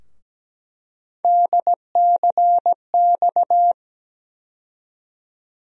Morse code, heard as a keyed tone, about 17 wpm, 700 Hz. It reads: DCX
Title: USA Call Sign Postfixes - 3 Letters 17wpm